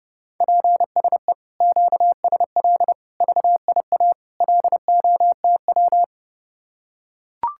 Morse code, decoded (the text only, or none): PSE QSL VIA LOTW